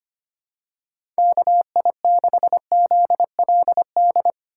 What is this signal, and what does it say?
Morse code, heard as a keyed tone, about 25 wpm, 700 Hz. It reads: KI6ZLD
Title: USA Call Signs - 4x Character Spacing - 25wpm